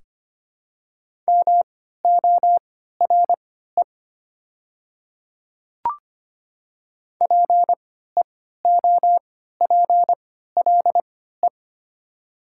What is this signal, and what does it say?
Morse code, heard as a keyed tone, about 25 wpm, spaced as for 14 wpm, 690 Hz. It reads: MORE PEOPLE
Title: 3x Character Spacing - Top 200 Words - 25wpm